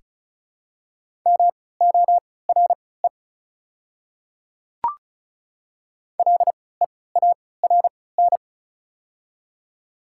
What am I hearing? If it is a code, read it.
Morse code, heard as a keyed tone, about 36 wpm, spaced as for 20 wpm, 700 Hz. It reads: MORE LEARN
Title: 3x Character Spacing - Top 300 Words - 35wpm